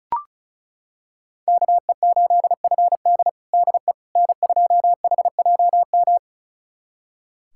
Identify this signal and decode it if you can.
Morse code, heard as a keyed tone, about 34 wpm, 700 Hz. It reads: KE8FD DE N2HJM
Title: USA Callsign DE Callsign - 5 Characters - 35wpm